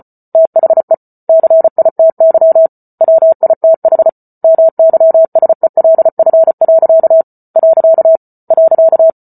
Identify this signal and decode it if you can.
Morse code, heard as a keyed tone, about 34 wpm, 650 Hz. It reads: THE CITY WITH MYSELF. .